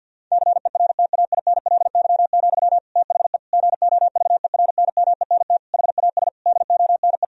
Morse code, heard as a keyed tone, about 50 wpm, 700 Hz. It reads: CERTAINLY, THE GOVERNMENT HAS DONE
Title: Sentences from Top 1000 Words - Review Part 3 - 50wpm